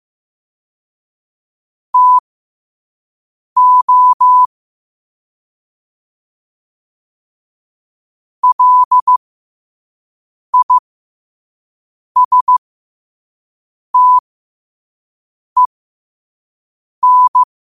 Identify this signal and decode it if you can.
Morse code, heard as a keyed tone, about 15 wpm, spaced as for 5 wpm, 1000 Hz. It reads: TO LISTEN
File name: qotdfiles / 5wpm